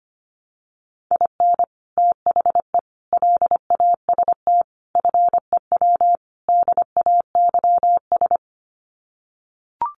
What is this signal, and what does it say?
Morse code, heard as a keyed tone, about 25 wpm, 700 Hz. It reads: IN THE LAST FEW DAYS
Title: Sets of 5 Words Encore2 25wpm